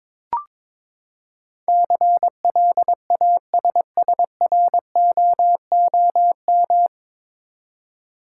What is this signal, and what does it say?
Morse code, heard as a keyed tone, about 22 wpm, 700 Hz. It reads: CLASSROOM
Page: Morse Code Ninja practice 22wpm